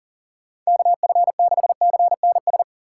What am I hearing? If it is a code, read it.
Morse code, heard as a keyed tone, about 49 wpm, 700 Hz. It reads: KF6CNS
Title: Speed Racing - USA 2x3 Call Signs 40wpm